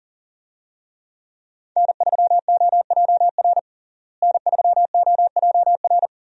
Morse code, heard as a keyed tone, about 40 wpm, 700 Hz. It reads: N3OJR N3OJR
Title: USA Call Signs - 5 Characters - Repeated 2x - 40wpm